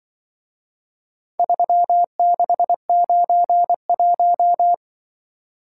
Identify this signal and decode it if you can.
Morse code, heard as a keyed tone, about 23 wpm, 700 Hz. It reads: 3691
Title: Speed-Racing - Four Letter-Number Combinations - 20wpm